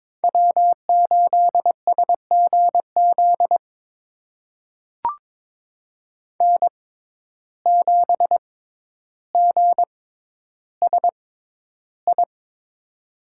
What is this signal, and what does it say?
Morse code, heard as a keyed tone, about 22 wpm, spaced as for 11 wpm, 700 Hz. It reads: W8SGZ N7GSI
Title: USA Call Signs - 5 Characters - 6x Character Spacing - 22wpm